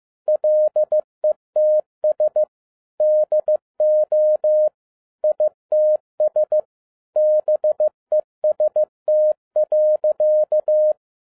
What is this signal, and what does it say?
Morse code, heard as a keyed tone, about 15 wpm, 610 Hz. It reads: LETS DO ITS BEST.
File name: 2012-01-mp3 / n13-A NEWLY DESIGNED QSL CARD-15wpm